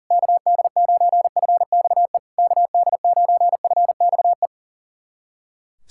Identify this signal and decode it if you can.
Morse code, heard as a keyed tone, about 40 wpm, 700 Hz. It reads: KD9FXE KD9FXE
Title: SOTA QSO Practice 40wpm